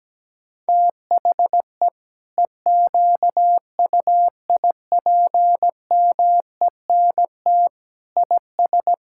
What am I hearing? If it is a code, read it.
Morse code, heard as a keyed tone, about 17 wpm, 700 Hz. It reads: THE EQUIPMENT IS
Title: Sentences from Top 1000 Words - Review Part 5 - 17wpm